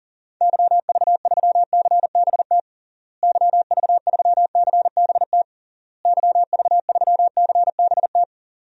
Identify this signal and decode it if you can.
Morse code, heard as a keyed tone, about 40 wpm, 700 Hz. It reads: YV3CBT YV3CBT YV3CBT
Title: DXCC-100 Call Signs - 6 Characters - Repeated 3x - 40wpm